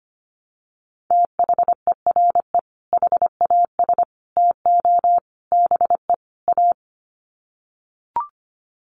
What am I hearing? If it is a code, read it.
Morse code, heard as a keyed tone, about 25 wpm, 700 Hz. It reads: THERE HAS TO BE A
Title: Sets of 5 Words Encore 25wpm